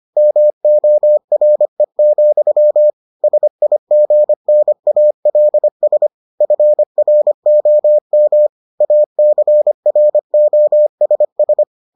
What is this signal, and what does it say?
Morse code, heard as a keyed tone, about 25 wpm, 600 Hz. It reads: MORE, SIGNALS FROM ACROSS